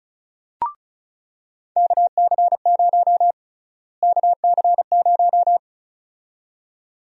Morse code, heard as a keyed tone, about 34 wpm, 700 Hz. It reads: KC0 KC0
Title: USA Call Sign Prefixes - Repeated 2x - 35wpm